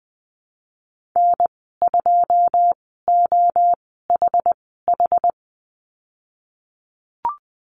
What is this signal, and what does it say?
Morse code, heard as a keyed tone, about 20 wpm, spaced as for 15 wpm, 700 Hz. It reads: N2OHH
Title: 088 Lesson 15g - Number 5 - Call Signs 20wpm